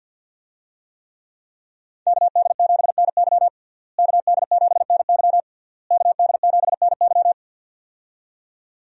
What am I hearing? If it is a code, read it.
Morse code, heard as a keyed tone, about 49 wpm, 700 Hz. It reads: KD7NY KD7NY KD7NY
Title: USA Call Signs - 5 Characters - Repeated 3x - 50wpm